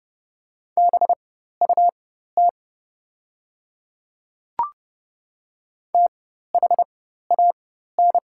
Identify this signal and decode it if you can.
Morse code, heard as a keyed tone, about 30 wpm, spaced as for 14 wpm, 700 Hz. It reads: BUT THAN
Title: 4x Character Spacing - Top 100 Words - 30wpm